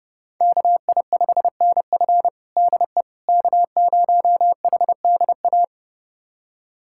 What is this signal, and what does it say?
Morse code, heard as a keyed tone, about 32 wpm, 700 Hz. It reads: KI5NF DE K0HDA
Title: USA Callsign DE Callsign - 5 Characters - 30wpm